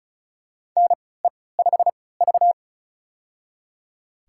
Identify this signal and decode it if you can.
Morse code, heard as a keyed tone, about 34 wpm, spaced as for 20 wpm, 700 Hz. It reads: NE5V
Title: USA Call Signs - 4 Characters - 3x Character Spacing - 35wpm